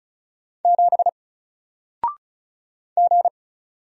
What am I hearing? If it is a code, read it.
Morse code, heard as a keyed tone, about 36 wpm, 700 Hz. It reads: 7 G
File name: Single-Letter-Number-35wpm